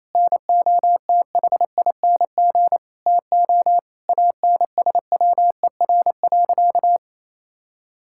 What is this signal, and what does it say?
Morse code, heard as a keyed tone, about 28 wpm, 700 Hz. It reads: NOTHING TO ANSWER.
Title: Sentences from Top 1600 Words 28wpm